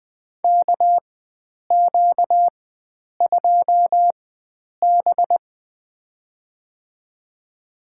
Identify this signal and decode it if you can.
Morse code, heard as a keyed tone, about 20 wpm, spaced as for 9 wpm, 700 Hz. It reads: KQ2B
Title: USA Call Signs - 4 Characters - 4x Character Spacing - 20wpm